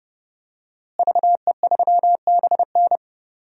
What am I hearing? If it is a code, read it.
Morse code, heard as a keyed tone, about 30 wpm, 700 Hz. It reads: VE3BN
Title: DXCC-100 Call Signs - 5 Characters - Repeated 2x - 30wpm